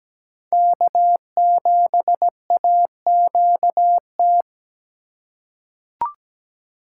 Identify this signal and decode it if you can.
Morse code, heard as a keyed tone, about 17 wpm, 700 Hz. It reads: K7AQT
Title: USA Call Signs - 5 Characters - 3x Character Spacing - 17wpm